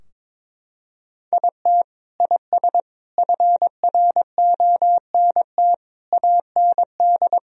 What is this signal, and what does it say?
Morse code, heard as a keyed tone, about 22 wpm, 700 Hz. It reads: IT IS FRONT AND
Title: Sentences from Top 1000 Words - Review Part 2 - 22wpm